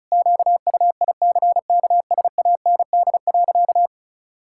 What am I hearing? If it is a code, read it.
Morse code, heard as a keyed tone, about 36 wpm, 700 Hz. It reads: QUICKSAND.
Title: Pangram Challenge v2 - 35wpm